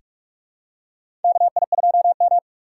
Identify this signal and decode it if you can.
Morse code, heard as a keyed tone, about 46 wpm, 700 Hz. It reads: KI2M
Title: USA Call Signs - 4 Characters - 8x Character Spacing - 45wpm